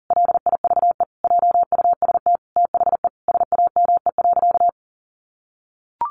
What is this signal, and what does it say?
Morse code, heard as a keyed tone, about 37 wpm, 700 Hz. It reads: LIVE JUST THE SAME.
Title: Sentences from Top 300 Words 40wpm